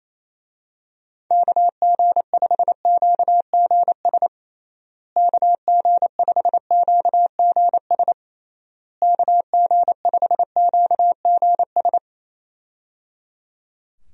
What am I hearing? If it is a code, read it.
Morse code, heard as a keyed tone, about 28 wpm, 700 Hz. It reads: KG5QGS KG5QGS KG5QGS
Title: USA Call Signs - 6 Characters - Repeated 3x - 28wpm